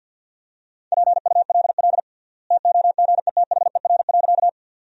Morse code, heard as a keyed tone, about 50 wpm, 700 Hz. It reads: PULL TOGETHER.
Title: 3x Word-Spacing - Sentences from Top 400 Words - 50wpm